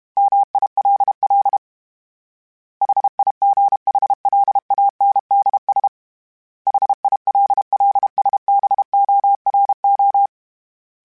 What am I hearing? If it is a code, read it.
Morse code, heard as a keyed tone, about 32 wpm, 800 Hz. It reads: MILL HIGHLANDS HILLSBORO
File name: fromSMs / FromSMs mp3s / Cities@30